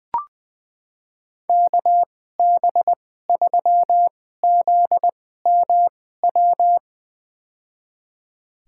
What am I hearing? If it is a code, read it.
Morse code, heard as a keyed tone, about 21 wpm, spaced as for 15 wpm, 700 Hz. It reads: KB3ZMW
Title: USA Call Signs - 2x Character Spacing - 20wpm